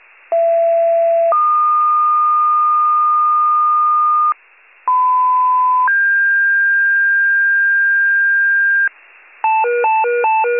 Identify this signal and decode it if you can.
Two-tone page, heard as a one-second tone, then a three-second tone, about 670 then 1220 Hz, about 1010 then 1600 Hz, with noise. It is heard from 0.3 to 8.9 s.
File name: wcalert